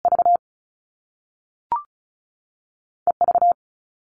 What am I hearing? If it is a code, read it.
Morse code, heard as a keyed tone, about 36 wpm, spaced as for 3 wpm, 690 Hz. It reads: VE V
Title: All Cut Numbers - 2-Digit 35wpm